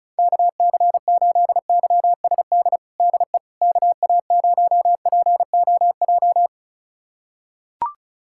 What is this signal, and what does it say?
Morse code, heard as a keyed tone, about 36 wpm, 700 Hz. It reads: KC8YSD DE KA0POJ
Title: USA Callsign DE Callsign - 6 Characters - 35wpm